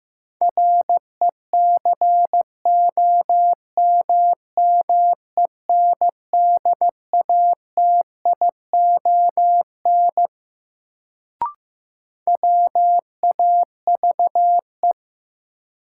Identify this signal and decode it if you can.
Morse code, heard as a keyed tone, about 15 wpm, 700 Hz. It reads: RECOMMENDATION WAVE
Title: Sentences from Top 1400 Words - New Word Drill - 15wpm